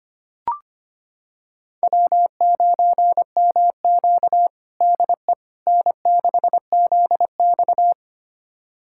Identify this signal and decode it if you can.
Morse code, heard as a keyed tone, about 25 wpm, 700 Hz. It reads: W9MQ DE N6ZX
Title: USA Callsign DE Callsign - 4 Characters - 25wpm